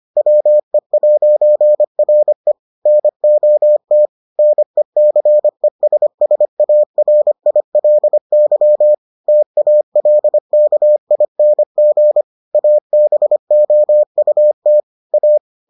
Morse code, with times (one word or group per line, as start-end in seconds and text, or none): 0.2-2.5 WE'RE
2.9-4.1 NOT
4.4-8.9 NECESSARILY
9.3-12.2 TALKING
12.5-14.8 ABOUT
15.1-15.4 A